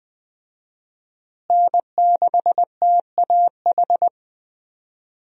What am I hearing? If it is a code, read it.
Morse code, heard as a keyed tone, about 20 wpm, 700 Hz. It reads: N6TAH
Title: USA Call Signs - 5 Characters - Repeated 2x - 20wpm